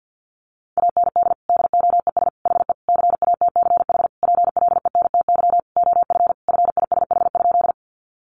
Morse code, heard as a keyed tone, about 50 wpm, 700 Hz. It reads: AND DOES HE CATCH PLENTY OF FISH?
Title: Sentences from Top 1500 Words 50wpm